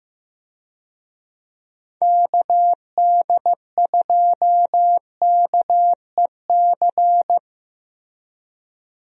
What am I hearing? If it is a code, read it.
Morse code, heard as a keyed tone, about 15 wpm, 700 Hz. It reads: KD2KEC
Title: USA Call Signs - 6 Characters - Repeated 2x - 15wpm